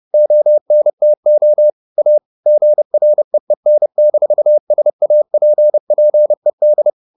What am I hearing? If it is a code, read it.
Morse code, heard as a keyed tone, about 30 wpm, 600 Hz. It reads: ONTO A GREEN-SAPPED
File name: the old man and the sea / the old man and the sea 1 30wpm